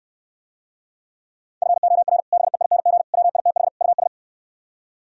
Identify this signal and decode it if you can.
Morse code, heard as a keyed tone, about 67 wpm, 700 Hz. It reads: FOR BEING WITH US